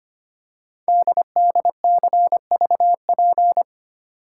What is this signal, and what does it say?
Morse code, heard as a keyed tone, about 25 wpm, 700 Hz. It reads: DDCVP
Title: Code Groups - 5 Letters - 25wpm